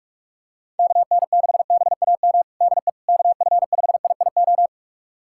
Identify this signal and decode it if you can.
Morse code, heard as a keyed tone, about 41 wpm, 700 Hz. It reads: KN6BAM DE KF5IIO
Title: USA Callsign DE Callsign - 6 Characters - 45wpm